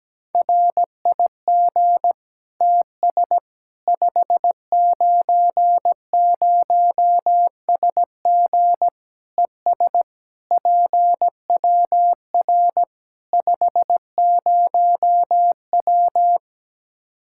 Morse code, heard as a keyed tone, about 17 wpm, 700 Hz. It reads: RIG TS 590SG ES PWR 50W
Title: K1USN SST QSO Practice 17wpm